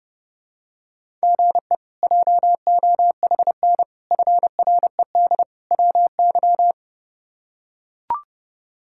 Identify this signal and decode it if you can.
Morse code, heard as a keyed tone, about 30 wpm, 700 Hz. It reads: GE JOHN FRED WY
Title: K1USN SST QSO Practice - 30wpm